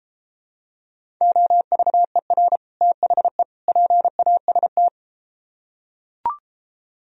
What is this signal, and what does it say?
Morse code, heard as a keyed tone, about 33 wpm, 700 Hz. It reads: OVER THE PAST